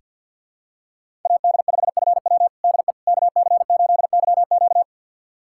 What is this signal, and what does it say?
Morse code, heard as a keyed tone, about 45 wpm, 700 Hz. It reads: AD5FW DE KC8YQ
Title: USA Callsign DE Callsign - 5 Characters - 50wpm